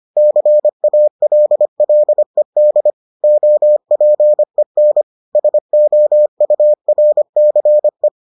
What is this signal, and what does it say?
Morse code, heard as a keyed tone, about 25 wpm, 600 Hz. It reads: CALLED OPEN SOURCE